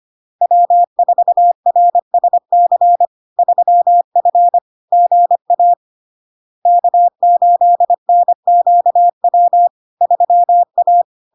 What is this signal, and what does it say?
Morse code, heard as a keyed tone, about 25 wpm, 700 Hz. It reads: W4RSC 3F GA K8NQW 3A
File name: LARK-FD-2022-Log-in-CW-25wpm-700-Hz